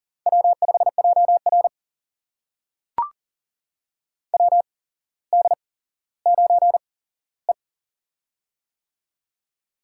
Morse code, heard as a keyed tone, about 42 wpm, spaced as for 17 wpm, 700 Hz. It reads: W5JR WD9E
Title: USA Call Signs - 4 Characters - 8x Character Spacing - 40wpm